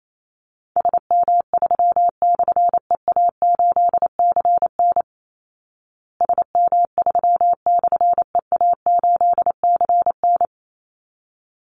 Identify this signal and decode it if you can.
Morse code, heard as a keyed tone, about 28 wpm, 700 Hz. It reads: SM3/EA8CN SM3/EA8CN
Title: Call Signs Repeated 3x - 28wpm